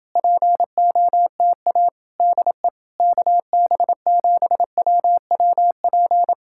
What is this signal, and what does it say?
Morse code, heard as a keyed tone, about 31 wpm, 700 Hz. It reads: POTA DE KB7WWP